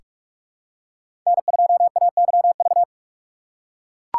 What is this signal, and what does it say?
Morse code, heard as a keyed tone, about 43 wpm, 700 Hz. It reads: N2AYV
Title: USA Call Signs - 4x Character Spacing - 45wpm